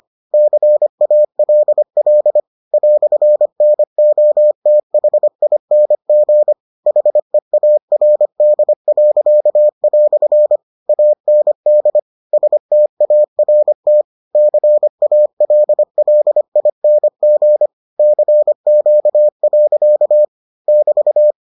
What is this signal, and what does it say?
Morse code, heard as a keyed tone, about 25 wpm, 600 Hz. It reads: CALL "NOTHING HEARD." AND START CALLING CQ. <BT>